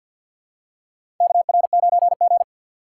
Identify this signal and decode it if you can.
Morse code, heard as a keyed tone, about 52 wpm, 690 Hz. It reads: KR9G
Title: USA Call Signs - 4x Character Spacing - 50wpm